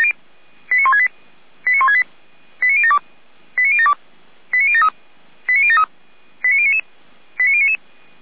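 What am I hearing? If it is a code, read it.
Five-tone selective call (ZVEI), heard as quick runs of five tones, tones of about 70 ms. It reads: 89168 89168 89072 89072 89073 89073 89099 89099
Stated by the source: ZVEI five-tone call received by SDR.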